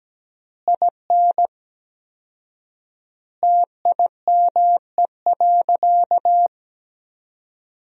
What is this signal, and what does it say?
Morse code, heard as a keyed tone, about 17 wpm, 700 Hz. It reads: IN TIME.
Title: 4x Word-Spacing - Sentences from Top 300 Words - 17wpm